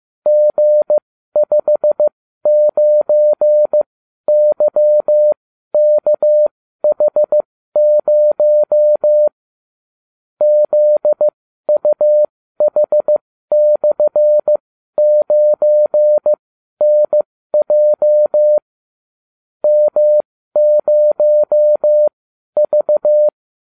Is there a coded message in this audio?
Yes: Morse code from 0.3 to 23.3 s, heard as a keyed tone, about 15 wpm, spaced as for 12 wpm, 600 Hz. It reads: G59YKH0 ZUH/9NJ M0V